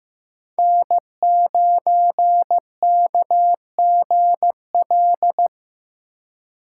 Morse code, heard as a keyed tone, about 15 wpm, 700 Hz. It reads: N9KGL
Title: USA Call Signs - 3x Character Spacing - 15wpm